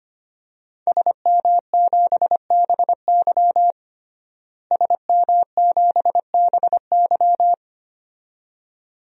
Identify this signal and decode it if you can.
Morse code, heard as a keyed tone, about 25 wpm, 700 Hz. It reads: SM7BY SM7BY
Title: DXCC-100 Call Signs - 5 Characters - Repeated 2x - 25wpm